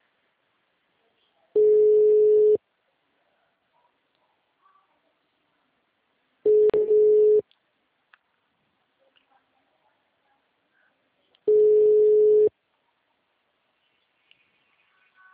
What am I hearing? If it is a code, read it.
Ringback tone (Europe, 425 Hz).